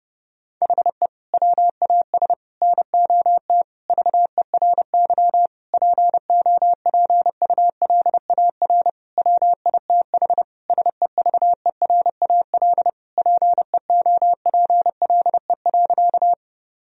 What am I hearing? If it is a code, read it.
Morse code, heard as a keyed tone, about 32 wpm, 700 Hz. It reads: HE WAS NOT VERY POPULAR WITH SEVERAL PEOPLE.
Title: Sentences from Top 1000 Words - Review Part 3 - 30wpm